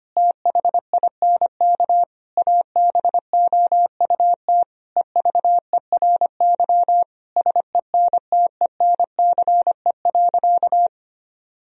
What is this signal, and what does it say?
Morse code, heard as a keyed tone, about 25 wpm, 700 Hz. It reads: THINK ABOUT EVERY SENTENCE.
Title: Sentences from Top 1000 Words - Review Part 1 - 25wpm